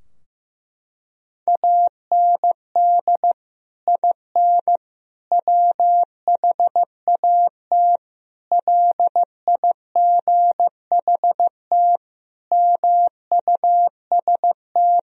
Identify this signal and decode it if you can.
Morse code, heard as a keyed tone, about 15 wpm, 700 Hz. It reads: AND IN WHAT LIGHT MUST
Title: Sentences from Top 1000 Words - Review Part 2 - 15wpm